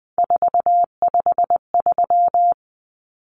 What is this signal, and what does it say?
Morse code, heard as a keyed tone, about 20 wpm, 700 Hz. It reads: 453
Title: Common Cut Numbers - 3-Digit 20wpm